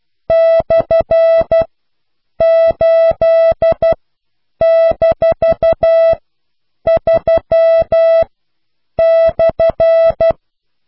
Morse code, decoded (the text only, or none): /8-3/